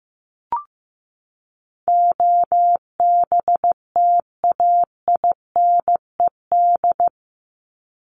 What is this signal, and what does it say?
Morse code, heard as a keyed tone, about 15 wpm, 700 Hz. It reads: OBTAINED
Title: ED Suffix - Top 100 Words with Base Words 15wpm